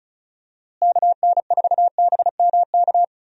Morse code, heard as a keyed tone, about 36 wpm, 700 Hz. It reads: KN4BMK
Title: USA Call Signs - 6 Characters - Repeated 3x - 35wpm